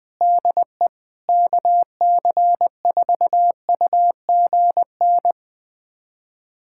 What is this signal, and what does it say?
Morse code, heard as a keyed tone, about 21 wpm, 700 Hz. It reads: DE KC4UGN
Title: USA Callsign DE Callsign 20wpm